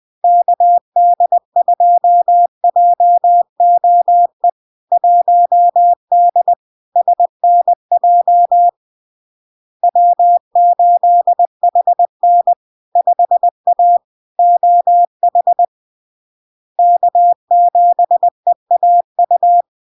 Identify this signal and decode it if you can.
Morse code, heard as a keyed tone, about 20 wpm, 700 Hz. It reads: KD2JOE 1D SNJ W8HN 5A OH K7EAU
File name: LARK-FD-2022-Log-in-CW-20wpm-700-Hz